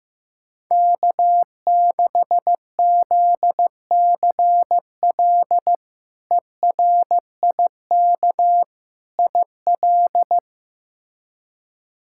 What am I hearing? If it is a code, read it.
Morse code, heard as a keyed tone, about 15 wpm, 700 Hz. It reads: K6ZCL ERIK IL
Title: K1USN SST QSO Practice - 15wpm